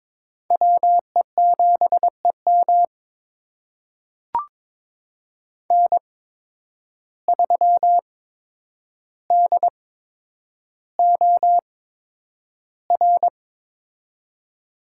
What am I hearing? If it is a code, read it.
Morse code, heard as a keyed tone, about 22 wpm, spaced as for 9 wpm, 700 Hz. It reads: WE7EM N3DOR
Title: USA Call Signs - 5 Characters - 8x Character Spacing - 22wpm